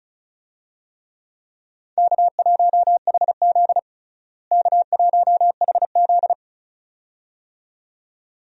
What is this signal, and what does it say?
Morse code, heard as a keyed tone, about 36 wpm, 700 Hz. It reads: K1HZ K1HZ
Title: USA Call Signs - 4 Characters - Repeated 2x - 35wpm